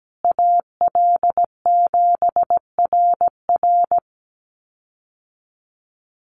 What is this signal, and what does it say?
Morse code, heard as a keyed tone, about 17 wpm, 700 Hz. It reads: AL7RR
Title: USA Call Signs Repeated 3x 17wpm